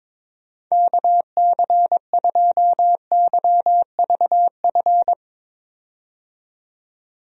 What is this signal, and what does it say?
Morse code, heard as a keyed tone, about 22 wpm, 700 Hz. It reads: KC2YVF
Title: USA Call Signs - 6 Characters - 22wpm